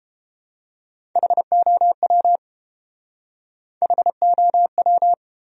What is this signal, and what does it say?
Morse code, heard as a keyed tone, about 31 wpm, 700 Hz. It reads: HOW HOW